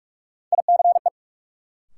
Morse code, heard as a keyed tone, about 47 wpm, 700 Hz. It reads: ICE
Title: Sentences from Top 1000 Words - New Word Drill - 45wpm